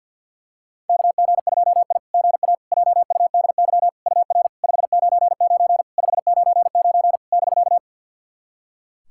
Morse code, heard as a keyed tone, about 50 wpm, 700 Hz. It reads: KG2I GA JUDY UR 599 599 <BK>
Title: SOTA QSO Practice 50wpm